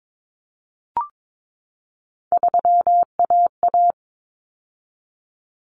Morse code, heard as a keyed tone, about 22 wpm, 700 Hz. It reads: 3AA
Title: Common Cut Numbers - 3-Digit 22wpm